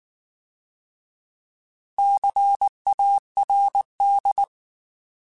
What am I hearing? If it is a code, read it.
Morse code, heard as a keyed tone, about 19 wpm, 800 Hz. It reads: CARD
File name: Alphabetical / Alphabetical mp3s / Cs @18